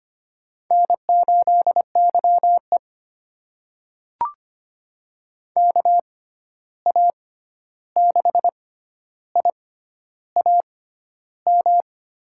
Morse code, heard as a keyed tone, about 25 wpm, spaced as for 11 wpm, 700 Hz. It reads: N8YE KA6IAM
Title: USA Call Signs - 6x Character Spacing - 25wpm